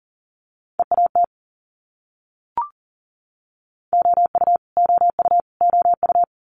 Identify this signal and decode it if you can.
Morse code, heard as a keyed tone, about 40 wpm, 700 Hz. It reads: EAT OU OU OU
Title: Multi-letter Phonemes Repeated 3x 40wpm